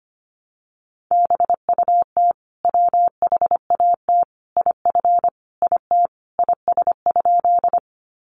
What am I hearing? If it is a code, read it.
Morse code, heard as a keyed tone, about 25 wpm, 700 Hz. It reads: BUT WHAT IF IT IS?
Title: Sentences from Top 100 Words 25wpm